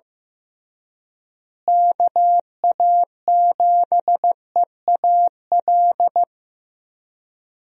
Morse code, heard as a keyed tone, about 15 wpm, 700 Hz. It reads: KA7EAL
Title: USA Call Signs - 6 Characters - Repeated 3x - 15wpm